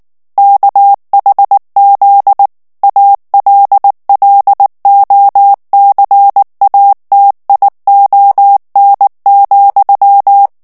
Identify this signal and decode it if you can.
Morse code, heard as a keyed tone, about 19 wpm, 800 Hz. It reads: KHZ ALLOCATION,
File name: ANGLAIS texte clair 3 vitesse 18 mots minute